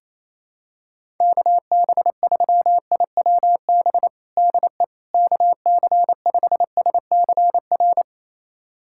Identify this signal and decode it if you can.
Morse code, heard as a keyed tone, about 28 wpm, 700 Hz. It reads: KB3IWB DE KC5SCR
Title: USA Callsign DE Callsign - 6 Characters - 28wpm